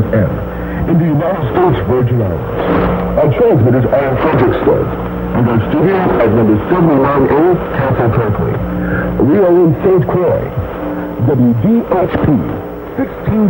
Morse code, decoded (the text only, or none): none